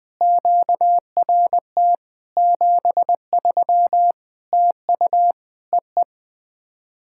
Morse code, heard as a keyed tone, about 20 wpm, 690 Hz. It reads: QRT 73 TU EE
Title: POTA QSO Practice - 20wpm